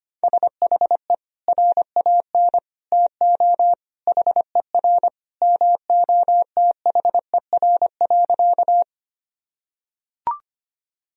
Morse code, heard as a keyed tone, about 25 wpm, 700 Hz. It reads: SHE RAN TO HER MOTHER.
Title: Sentences from Top 1000 Words - Review Part 3 - 25wpm